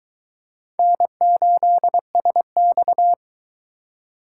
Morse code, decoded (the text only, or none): N8SX